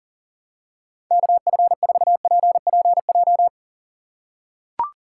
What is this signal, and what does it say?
Morse code, heard as a keyed tone, about 40 wpm, 700 Hz. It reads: KF4PPJ
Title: USA Call Signs - 3x Character Spacing - 40wpm